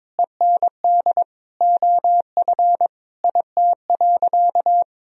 Morse code, heard as a keyed tone, about 21 wpm, 700 Hz. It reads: END OF IT.
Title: Sentences from Top 200 Words and Abbreviations 22wpm